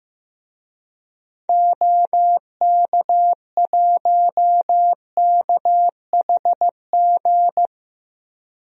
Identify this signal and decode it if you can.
Morse code, heard as a keyed tone, about 15 wpm, 700 Hz. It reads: OK1KHG
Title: DXCC-100 Call Signs - 6 Characters - 15wpm